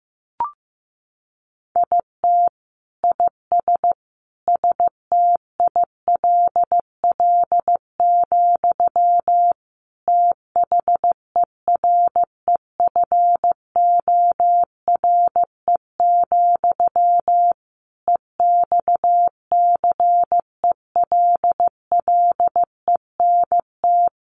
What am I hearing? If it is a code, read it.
Morse code, heard as a keyed tone, about 15 wpm, 700 Hz. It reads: IT IS STILL, THEREFORE, EXCELLENT
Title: Sentences from Top 1000 Words 15wpm